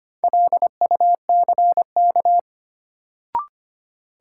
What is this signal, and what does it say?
Morse code, heard as a keyed tone, about 25 wpm, 700 Hz. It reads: LUCK
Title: Sentences from Top 1500 Words - New Word Drill Alt - 25wpm